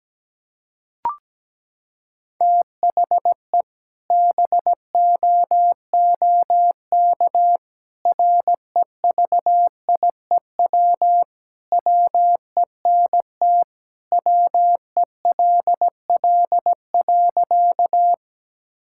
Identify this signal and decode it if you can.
Morse code, heard as a keyed tone, about 17 wpm, 700 Hz. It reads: THE BOOK REVIEW WENT WELL.
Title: Sentences from Top 1000 Words - Review Part 6 - 17wpm